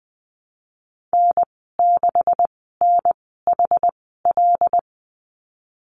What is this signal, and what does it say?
Morse code, heard as a keyed tone, about 19 wpm, spaced as for 14 wpm, 700 Hz. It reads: N6NHL
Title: 179 Lesson 29g - Slash - Call Signs 20wpm